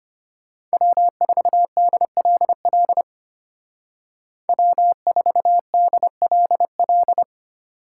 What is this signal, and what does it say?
Morse code, heard as a keyed tone, about 33 wpm, 700 Hz. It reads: W4DLL W4DLL